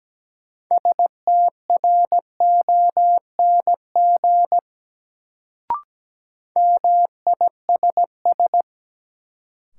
Morse code, heard as a keyed tone, about 17 wpm, 700 Hz. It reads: STRONG MISS
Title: Sentences from Top 500 Words - New Word Drill - 17wpm